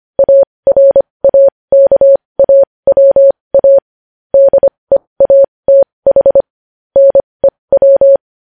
Morse code, heard as a keyed tone, about 25 wpm, spaced as for 20 wpm, 550 Hz. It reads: ARAKAWA DEATH NEW
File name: podcast / News headlines-1x-25x20